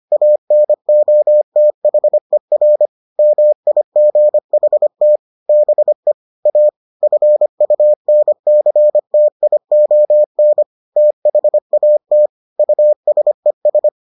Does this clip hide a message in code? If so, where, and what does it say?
Yes: Morse code from 0.1 to 13.9 s, heard as a keyed tone, about 25 wpm, 600 Hz. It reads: ANOTHER MIGHT BE A FUNCTION THAT USES